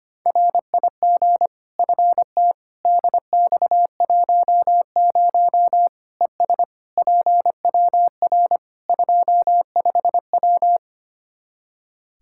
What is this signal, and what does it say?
Morse code, heard as a keyed tone, about 25 wpm, 700 Hz. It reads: RIG FT DX10 ES PWR 25W
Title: Practice QSO - Exchange 2 - Info - 25wpm